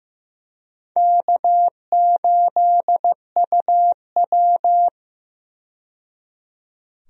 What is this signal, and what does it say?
Morse code, heard as a keyed tone, about 15 wpm, 700 Hz. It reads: K8UW
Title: USA Call Signs - 4 Characters - 15wpm